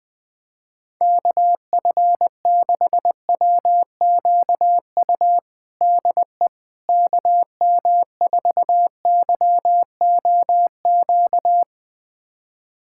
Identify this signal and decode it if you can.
Morse code, heard as a keyed tone, about 21 wpm, 700 Hz. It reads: KF6WQU DE KM4YOQ
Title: USA Callsign DE Callsign - 6 Characters - 20wpm